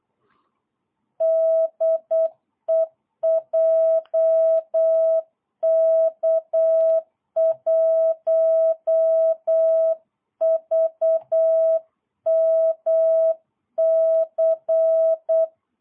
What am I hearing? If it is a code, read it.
Morse code, heard as a keyed tone, about 8 wpm, 650 Hz. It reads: DE JK1VMC
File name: jh8cbh / 2022cw